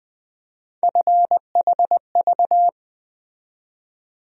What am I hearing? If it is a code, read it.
Morse code, heard as a keyed tone, about 20 wpm, 700 Hz. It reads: FHV
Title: USA Call Sign Postfixes - 3 Letters 20wpm